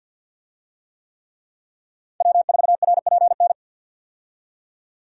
Morse code, heard as a keyed tone, about 48 wpm, 700 Hz. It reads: W4RPN
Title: USA Call Signs - 5 Characters - 50wpm